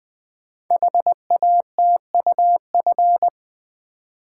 Morse code, heard as a keyed tone, about 20 wpm, 700 Hz. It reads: HATUF
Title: Code Groups - 5 Letters - 20wpm